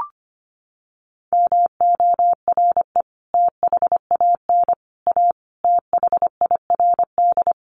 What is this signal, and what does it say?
Morse code, heard as a keyed tone, about 25 wpm, 700 Hz. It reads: MORE THAN A THIRD
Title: Sets of 5 Words Encore2 25wpm